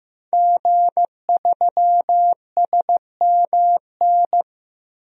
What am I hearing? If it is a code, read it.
Morse code, heard as a keyed tone, about 15 wpm, 700 Hz. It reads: G3SMN
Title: DXCC-100 Call Signs - 5 Characters - Repeated 3x - 15wpm